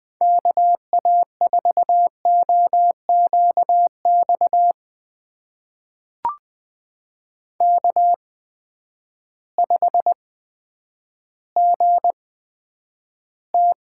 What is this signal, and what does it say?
Morse code, heard as a keyed tone, about 20 wpm, spaced as for 10 wpm, 700 Hz. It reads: KA4OQX K5GT
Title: USA Call Signs - 8x Character Spacing - 20wpm